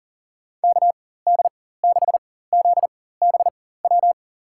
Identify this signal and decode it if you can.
Morse code, heard as a keyed tone, about 40 wpm, spaced as for 19 wpm, 700 Hz. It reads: KD6ZBW
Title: USA Call Signs - 4x Character Spacing - 40wpm